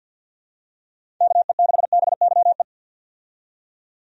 Morse code, heard as a keyed tone, about 47 wpm, 700 Hz. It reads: KE6BYE
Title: USA Call Signs - 6 Characters - 50wpm